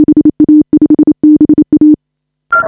Morse code, heard as a keyed tone, about 27 wpm, 340 Hz. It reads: HA5BA
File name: AUDIO2019 / HA5BA CW